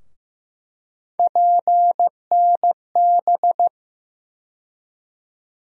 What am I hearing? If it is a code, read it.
Morse code, heard as a keyed tone, about 15 wpm, 700 Hz. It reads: PNB
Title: USA Call Sign Postfixes - 3 Letters 15wpm